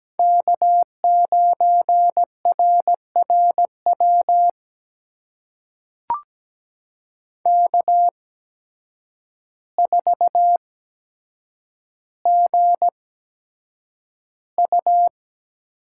Morse code, heard as a keyed tone, about 17 wpm, spaced as for 8 wpm, 700 Hz. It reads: K9RRW K4GU
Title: USA Call Signs - 8x Character Spacing - 17wpm